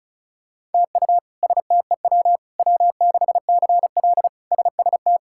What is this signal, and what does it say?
Morse code, heard as a keyed tone, about 36 wpm, 700 Hz. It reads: TU STEW W6CL SST
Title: K1USN SST QSO Practice - 35wpm